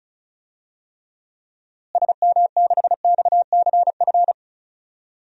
Morse code, heard as a keyed tone, about 34 wpm, 700 Hz. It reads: SM6XCF
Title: DXCC-100 Call Signs - 6 Characters - Repeated 3x - 35wpm